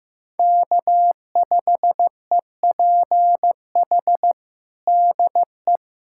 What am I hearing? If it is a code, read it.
Morse code, heard as a keyed tone, about 15 wpm, 700 Hz. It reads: K5EPH DE
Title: USA Callsign DE Callsign 15wpm